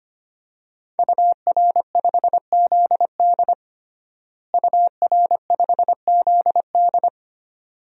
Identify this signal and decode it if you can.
Morse code, heard as a keyed tone, about 25 wpm, 700 Hz. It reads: UR5ZD UR5ZD
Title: DXCC-100 Call Signs - 5 Characters - Repeated 2x - 25wpm